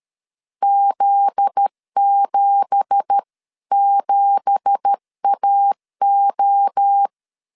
Morse code, heard as a keyed tone, about 13 wpm, 800 Hz. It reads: Z77AO
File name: LEZIONI / 9 2867 12wpm 2322